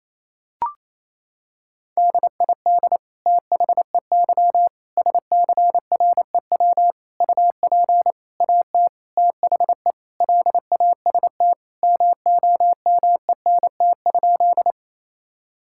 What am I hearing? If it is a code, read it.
Morse code, heard as a keyed tone, about 28 wpm, 700 Hz. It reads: DID THEY SCREW UP AT THE LAST MOMENT?
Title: Sentences from Top 1600 Words 28wpm